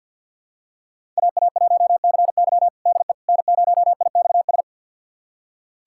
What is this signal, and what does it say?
Morse code, heard as a keyed tone, about 46 wpm, 700 Hz. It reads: AA1CY DE N0IXS
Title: USA Callsign DE Callsign - 5 Characters - 50wpm